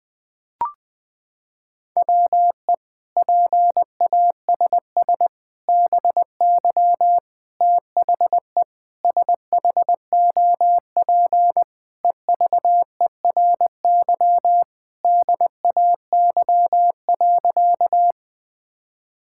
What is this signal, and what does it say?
Morse code, heard as a keyed tone, about 20 wpm, 700 Hz. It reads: WE PASS BY THE SHOP EVERY DAY.
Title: Sentences from Top 1600 Words 20wpm